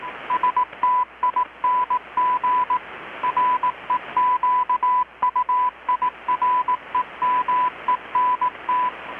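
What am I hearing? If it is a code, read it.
Morse code, heard as a keyed tone, about 18 wpm, 1000 Hz, with radio noise. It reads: STING REQUIREMENT
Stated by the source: Morse from KPH's Night of Nights XX maritime radiotelegraphy event, received in Chiapas, Mexico about 3,800 km away on a Kenwood R-600 with a grounded 10 m long-wire antenna.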